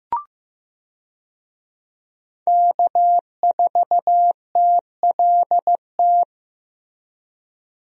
Morse code, heard as a keyed tone, about 15 wpm, 700 Hz. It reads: K4TLT
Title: USA Call Signs - 5 Characters - Repeated 3x - 15wpm